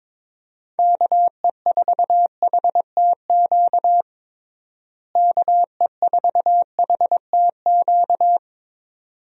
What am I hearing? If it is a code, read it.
Morse code, heard as a keyed tone, about 22 wpm, 700 Hz. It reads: KE4HTQ KE4HTQ
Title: USA Call Signs - 6 Characters - Repeated 2x - 22wpm